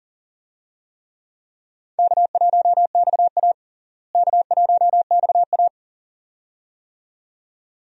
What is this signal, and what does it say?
Morse code, heard as a keyed tone, about 37 wpm, 700 Hz. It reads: K1XA K1XA
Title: USA Call Signs - 4 Characters - Repeated 2x - 40wpm